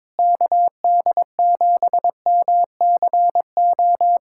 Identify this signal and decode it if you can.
Morse code, heard as a keyed tone, about 22 wpm, 700 Hz. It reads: KD7MCO
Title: USA Callsign DE Callsign 22wpm